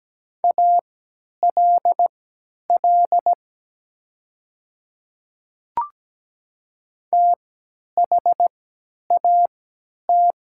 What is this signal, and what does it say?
Morse code, heard as a keyed tone, about 17 wpm, spaced as for 10 wpm, 700 Hz. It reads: ALL THAT
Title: 3x Character Spacing - Top 100 Words - 17wpm